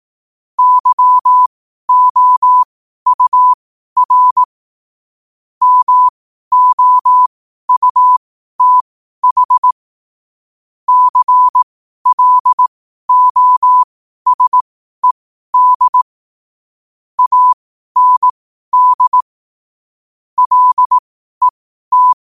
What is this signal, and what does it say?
Morse code, heard as a keyed tone, about 18 wpm, spaced as for 12 wpm, 1000 Hz. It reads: YOUR MOUTH CLOSED AND LET
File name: qotdfiles / 12wpm